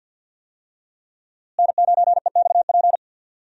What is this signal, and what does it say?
Morse code, heard as a keyed tone, about 47 wpm, 700 Hz. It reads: N9EXP
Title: USA Call Signs - 5 Characters - Repeated 3x - 50wpm